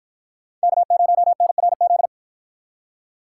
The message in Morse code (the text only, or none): K0NRZ